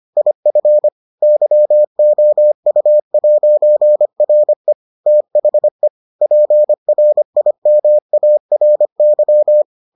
Morse code, heard as a keyed tone, about 25 wpm, 600 Hz. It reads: IF YOU'RE THE PRIMARY